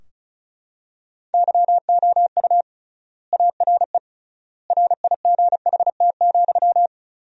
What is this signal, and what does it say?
Morse code, heard as a keyed tone, about 34 wpm, 700 Hz. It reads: YOU ARE RIGHT,
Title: 3x Word-Spacing - Sentences from Top 400 Words - 35wpm